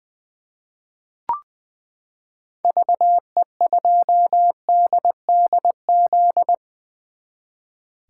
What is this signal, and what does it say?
Morse code, heard as a keyed tone, about 21 wpm, 700 Hz. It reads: VE2DDZ
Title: 2023 SOTA Chaser Call Signs - 20wpm